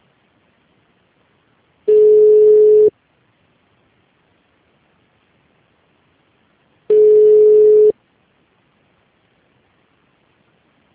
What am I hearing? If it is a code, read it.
Ringback tone (Europe, 425 Hz).
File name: out- -2022- - - .2770